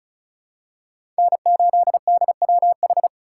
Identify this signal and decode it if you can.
Morse code, heard as a keyed tone, about 34 wpm, 700 Hz. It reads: N8DWH
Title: USA Call Signs - 5 Characters - 4x Character Spacing - 35wpm